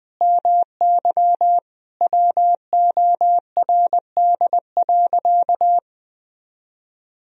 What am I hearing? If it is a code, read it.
Morse code, heard as a keyed tone, about 21 wpm, 700 Hz. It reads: MY WORD.
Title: Sentences from Top 1000 Words - Review Part 2 - 20wpm